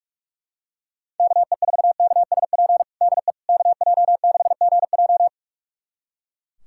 Morse code, heard as a keyed tone, about 41 wpm, 700 Hz. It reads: KE4KSP DE KJ6GJ
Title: USA Callsign DE Callsign 45wpm